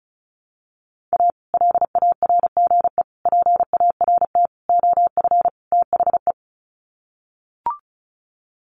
Morse code, heard as a keyed tone, about 34 wpm, 700 Hz. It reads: A LARGE PART OF THE
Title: Sets of 5 Words Encore2 35wpm